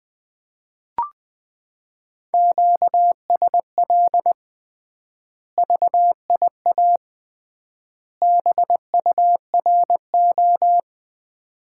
Morse code, heard as a keyed tone, about 20 wpm, 700 Hz. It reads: QSL VIA BURO
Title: Common QSO Phrases - 3x Word-Spacing - 20wpm